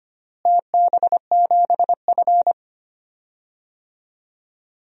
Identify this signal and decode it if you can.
Morse code, heard as a keyed tone, about 25 wpm, 700 Hz. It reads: TB7F
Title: DXCC-100 Call Signs - 4 Characters - Repeated 3x - 25wpm